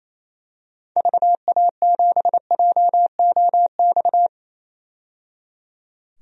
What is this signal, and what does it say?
Morse code, heard as a keyed tone, about 28 wpm, 700 Hz. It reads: VA7JOX
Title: DXCC-100 Call Signs - 6 Characters - 28wpm